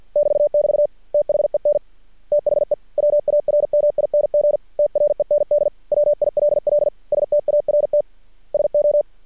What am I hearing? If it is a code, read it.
Morse code, heard as a keyed tone, about 53 wpm, 590 Hz. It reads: <BT><BT> THEN THE WARMING TREND WILL START SO
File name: archived / CRSnet